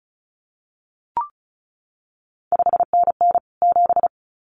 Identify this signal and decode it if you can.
Morse code, heard as a keyed tone, about 36 wpm, 700 Hz. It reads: 5NN 7
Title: CQ WW DX Contest Exchange Encore - 35wpm